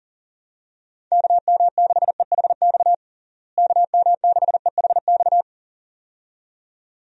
Morse code, heard as a keyed tone, about 40 wpm, 700 Hz. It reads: KM6EHX KM6EHX
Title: USA Call Signs - 6 Characters - Repeated 2x - 40wpm